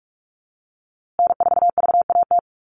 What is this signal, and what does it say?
Morse code, heard as a keyed tone, about 44 wpm, 700 Hz. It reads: N4VAT
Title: USA Call Signs Repeated 3x 45wpm